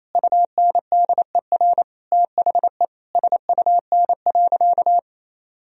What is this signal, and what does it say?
Morse code, heard as a keyed tone, about 28 wpm, 700 Hz. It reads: UNDER THE SUN.
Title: Sentences from Top 1000 Words - Review Part 1 - 28wpm